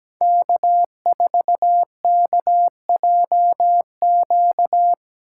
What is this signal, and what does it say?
Morse code, heard as a keyed tone, about 17 wpm, 700 Hz. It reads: K4KJQ
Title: USA Call Signs - 5 Characters - 2x Character Spacing - 17wpm